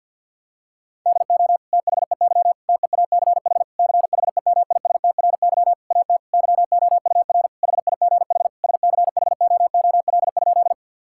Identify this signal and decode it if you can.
Morse code, heard as a keyed tone, about 50 wpm, 700 Hz. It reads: DO THEY TEACH CHEMISTRY AT YOUR HIGH SCHOOL?
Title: Sentences from Top 1000 Words - Review Part 6 - 50wpm